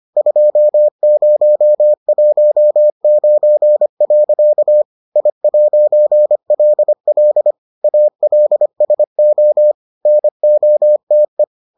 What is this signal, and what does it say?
Morse code, heard as a keyed tone, about 25 wpm, 600 Hz. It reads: 2019. I'LL ALSO NOTE